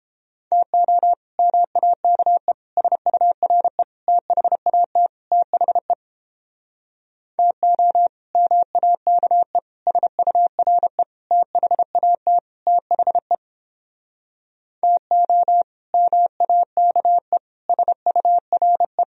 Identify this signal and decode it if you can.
Morse code, heard as a keyed tone, about 29 wpm, 700 Hz. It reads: TO MAKE SURE THAT THE TO MAKE SURE THAT THE TO MAKE SURE